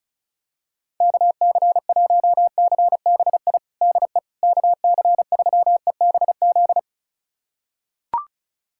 Morse code, heard as a keyed tone, about 34 wpm, 700 Hz. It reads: KC1CBI DE KC3EBZ
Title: USA Callsign DE Callsign - 6 Characters - 35wpm